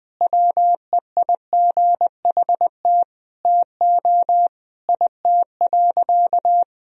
Morse code, heard as a keyed tone, about 21 wpm, 700 Hz. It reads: WEIGHT TO IT.
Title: Sentences from Top 1000 Words - Review Part 3 - 20wpm